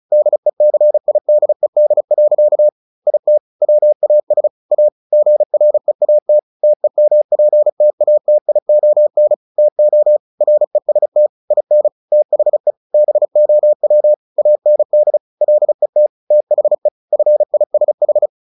Morse code, heard as a keyed tone, about 34 wpm, 600 Hz. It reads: DECIDED. IT WAS A GREAT TEMPTATION TO REST IN THE BOW AND LET THE FISH
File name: the old man and the sea 2 35wpm